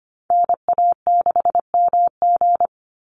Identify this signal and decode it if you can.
Morse code, heard as a keyed tone, about 25 wpm, 700 Hz. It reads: NA6MG
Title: NA SOTA Chaser Callsigns 25wpm